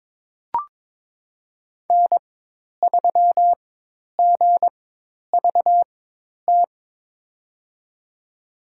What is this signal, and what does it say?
Morse code, heard as a keyed tone, about 21 wpm, spaced as for 10 wpm, 700 Hz. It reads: N3GVT
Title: USA Call Signs - 5 Characters - 4x Character Spacing - 22wpm